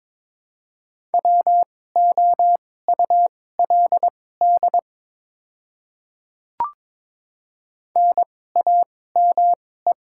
Morse code, heard as a keyed tone, about 22 wpm, spaced as for 16 wpm, 700 Hz. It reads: WOULD NAME
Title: 2x Character Spacing - Top 200 Words - 22wpm